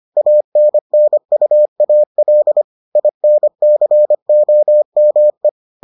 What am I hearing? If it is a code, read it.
Morse code, heard as a keyed tone, about 25 wpm, 600 Hz. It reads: ANNUAL INCOME